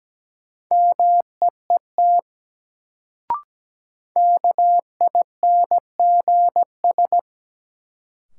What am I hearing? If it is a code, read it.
Morse code, heard as a keyed tone, about 17 wpm, 700 Hz. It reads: MEET KINGS
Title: Sentences from Top 1000 Words - Review Part 6 - New Word Drill - 17wpm